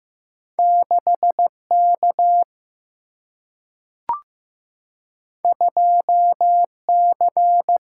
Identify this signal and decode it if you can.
Morse code, heard as a keyed tone, about 15 wpm, 700 Hz. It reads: 6 K 2 C
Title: Single-Letter Single-Number 15wpm